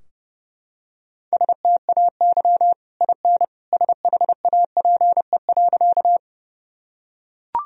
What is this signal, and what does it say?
Morse code, heard as a keyed tone, about 30 wpm, 700 Hz. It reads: STAY IN SHAPE.
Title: Sentences from Top 1000 Words - Review Part 3 - 30wpm